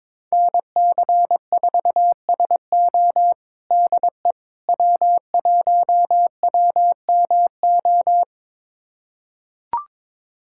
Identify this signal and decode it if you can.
Morse code, heard as a keyed tone, about 22 wpm, 700 Hz. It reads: NC4SO DE W1WMO
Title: USA Callsign DE Callsign - 5 Characters - 22wpm